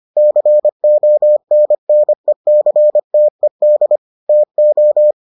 Morse code, heard as a keyed tone, about 25 wpm, 600 Hz. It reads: CONNECTED TO